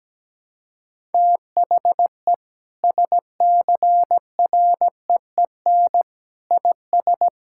Morse code, heard as a keyed tone, about 17 wpm, 700 Hz. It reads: THE SCREEN IS
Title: Sentences from Top 1000 Words - Review Part 6 - 17wpm